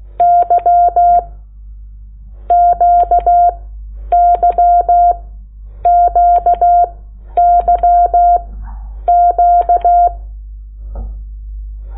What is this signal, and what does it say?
Morse code, heard as a keyed tone, about 16 wpm, 700 Hz. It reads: Y Q Y Q Y Q
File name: Lecciones / CLASE Nº8